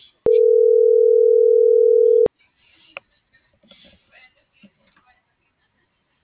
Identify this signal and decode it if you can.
Ringback tone (North America), 440 + 480 Hz.